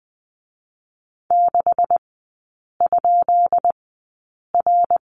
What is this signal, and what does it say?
Morse code, heard as a keyed tone, about 20 wpm, spaced as for 8 wpm, 700 Hz. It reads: 6?R
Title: 218 Lesson 35e - Number 0 - Three Characters 20wpm